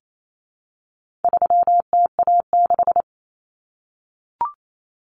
Morse code, heard as a keyed tone, about 28 wpm, 700 Hz. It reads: 3TA6
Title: Common Cut Numbers - 4-Digit 28wpm